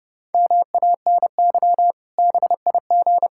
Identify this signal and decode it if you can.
Morse code, heard as a keyed tone, about 30 wpm, 700 Hz. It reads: MANY BIG
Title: Pangram Challenge v2 - 30wpm